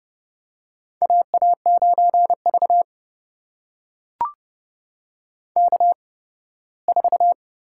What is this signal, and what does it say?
Morse code, heard as a keyed tone, about 30 wpm, spaced as for 18 wpm, 700 Hz. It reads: AA9V K4
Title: USA Call Signs - 4 Characters - 8x Character Spacing - 30wpm